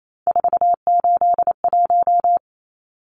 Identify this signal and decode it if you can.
Morse code, heard as a keyed tone, about 28 wpm, 700 Hz. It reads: 481
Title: Numbers - 3-Digit 28wpm